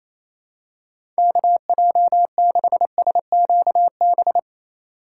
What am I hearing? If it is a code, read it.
Morse code, heard as a keyed tone, about 28 wpm, 700 Hz. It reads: KJ6SQB
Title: USA Call Signs - 6 Characters - 4x Character Spacing - 28wpm